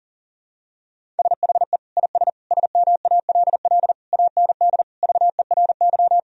Morse code, heard as a keyed tone, about 40 wpm, 700 Hz. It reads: SHE IS SMALL AND VERY
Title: Sentences from Top 1000 Words - Review Part 1 - 40wpm